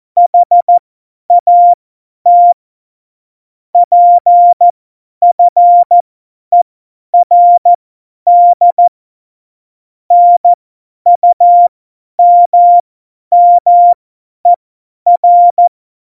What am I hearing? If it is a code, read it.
Morse code, heard as a keyed tone, about 14 wpm, spaced as for 10 wpm, 700 Hz. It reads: HAT PFERD NUMMER
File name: Morsekurs / Witze / Witz 11 August 2020, 50 F70